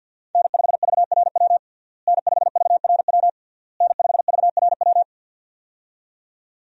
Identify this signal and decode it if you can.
Morse code, heard as a keyed tone, about 52 wpm, 690 Hz. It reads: N5VRW N5VRW N5VRW
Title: USA Call Signs - 5 Characters - Repeated 3x - 50wpm